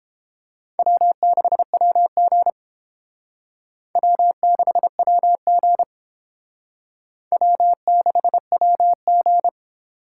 Morse code, heard as a keyed tone, about 29 wpm, 700 Hz. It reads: W6WG W6WG W6WG